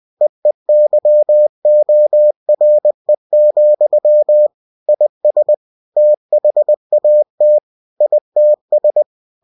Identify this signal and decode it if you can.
Morse code, heard as a keyed tone, about 20 wpm, 600 Hz. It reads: EEYORE, IS THAT ITS
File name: cw-books / winnie-the-pooh / winnie-20wpm-page-0079